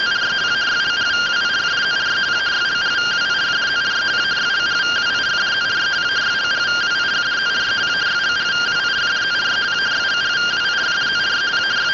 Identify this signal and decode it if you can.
Two-tone teleprinter signal, about 170 Hz shift.